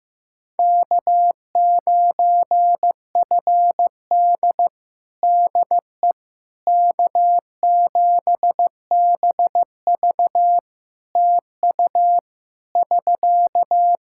Morse code, heard as a keyed tone, about 15 wpm, 700 Hz. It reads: K9FD DE K7BV TU <SK>
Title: Practice QSO - Exchange 4 - Ending - 15wpm